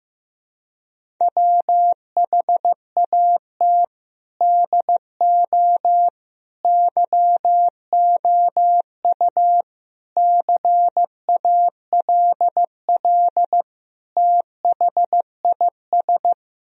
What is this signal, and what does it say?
Morse code, heard as a keyed tone, about 15 wpm, 700 Hz. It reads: WHAT DO YOU CALL THIS
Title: Sentences from Top 1000 Words - Review Part 3 - 15wpm